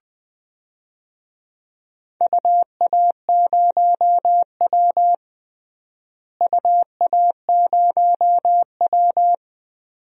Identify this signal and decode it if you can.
Morse code, heard as a keyed tone, about 21 wpm, 700 Hz. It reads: UA0W UA0W
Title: DXCC-100 Call Signs - 4 Characters - Repeated 2x - 20wpm